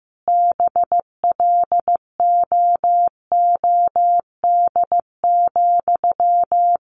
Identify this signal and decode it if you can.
Morse code, heard as a keyed tone, about 15 wpm, 700 Hz. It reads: BLOOD,
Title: Sentences from Top 1100 Words 15wpm